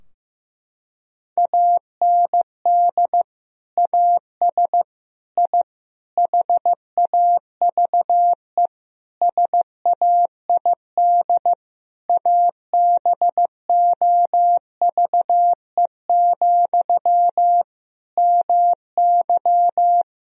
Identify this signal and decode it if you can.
Morse code, heard as a keyed tone, about 16 wpm, 700 Hz. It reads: AND AS I HAVE SAID ABOVE, MY
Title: Sentences from Top 1000 Words - Review Part 2 - 15wpm